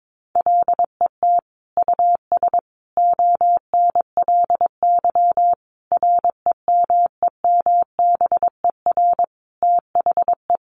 Morse code, heard as a keyed tone, about 22 wpm, 700 Hz. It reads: LET US ONLY REMEMBER THE
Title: Sentences from Top 800 Words 22wpm